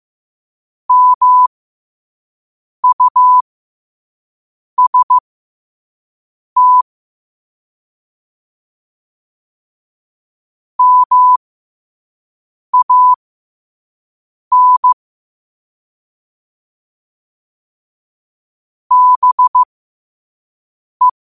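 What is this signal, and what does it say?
Morse code, heard as a keyed tone, about 15 wpm, spaced as for 5 wpm, 1000 Hz. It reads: MUST MAN BE
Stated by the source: Morse code practice at 5 wpm.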